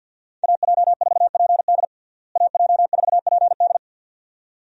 Morse code, heard as a keyed tone, about 47 wpm, 700 Hz. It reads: AJ4PD AJ4PD
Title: USA Call Signs - 5 Characters - Repeated 2x - 50wpm